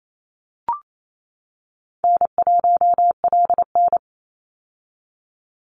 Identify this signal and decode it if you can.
Morse code, heard as a keyed tone, about 28 wpm, 700 Hz. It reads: N1LN
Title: Easy Call Signs - 28wpm